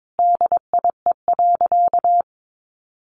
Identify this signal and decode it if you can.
Morse code, heard as a keyed tone, about 22 wpm, 700 Hz. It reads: DIE.
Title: Sentences from Top 1300 Words 22wpm